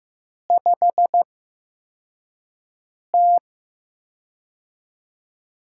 Morse code, heard as a keyed tone, about 15 wpm, spaced as for 4 wpm, 700 Hz. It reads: 5T
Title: USA Call Signs - 4 Characters - 8x Character Spacing - 15wpm